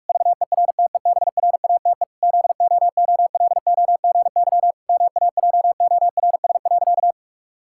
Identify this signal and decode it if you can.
Morse code, heard as a keyed tone, about 46 wpm, 700 Hz. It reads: VERTEBRATE ZOOLOGY MAJORS.
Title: Pangram Challenge v2 - 45wpm